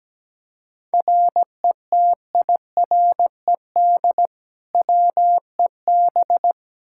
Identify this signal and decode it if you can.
Morse code, heard as a keyed tone, about 17 wpm, 700 Hz. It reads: RETIRED WEB
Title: K1USN SST QSO Practice 17wpm